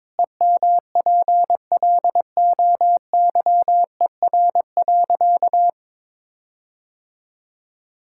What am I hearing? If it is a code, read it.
Morse code, heard as a keyed tone, about 22 wpm, 700 Hz. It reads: EMPLOYER.
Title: Sentences from Top 1600 Words 22wpm